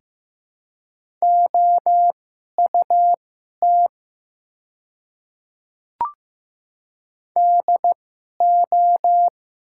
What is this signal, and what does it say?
Morse code, heard as a keyed tone, about 15 wpm, spaced as for 11 wpm, 690 Hz. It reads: OUT DO
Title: 2x Character Spacing - Top 100 Words - 15wpm